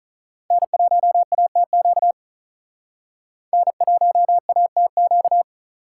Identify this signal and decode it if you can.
Morse code, heard as a keyed tone, about 43 wpm, 700 Hz. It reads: N1ATQ N1ATQ
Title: Speed Racing - USA Call Signs 35wpm